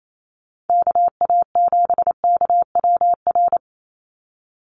Morse code, heard as a keyed tone, about 28 wpm, 700 Hz. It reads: KA7KWR
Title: USA Call Signs 28wpm